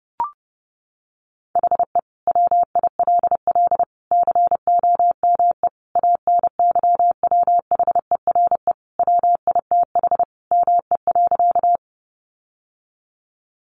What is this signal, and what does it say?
Morse code, heard as a keyed tone, about 30 wpm, 700 Hz. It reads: HE WILL COME ANYWHERE WITH ME.
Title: Sentences from Top 1400 Words 30wpm